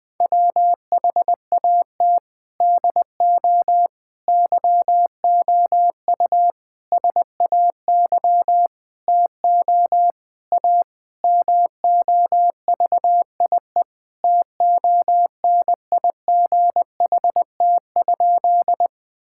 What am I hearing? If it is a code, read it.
Morse code, heard as a keyed tone, about 20 wpm, 700 Hz. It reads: WHAT DO YOU SAY TO A MOVIE TONIGHT?
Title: Sentences with 5-Word Phrases - 20wpm